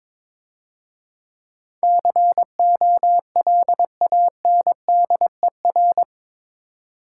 Morse code, heard as a keyed tone, about 24 wpm, 690 Hz. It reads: COLANDER
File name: Easy Words - Speed-Racing with Hints - 22wpm